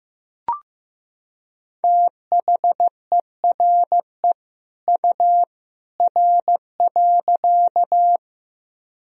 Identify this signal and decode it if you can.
Morse code, heard as a keyed tone, about 15 wpm, 700 Hz. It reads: THERE U R.
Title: Sentences from Top 100 Words and Abbreviations 15wpm